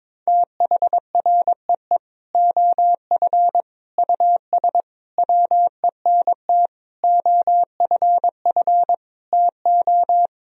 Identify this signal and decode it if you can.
Morse code, heard as a keyed tone, about 22 wpm, 700 Hz. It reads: THREE OF US WENT OFF TO
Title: Sentences from Top 1000 Words - Review Part 1 - 22wpm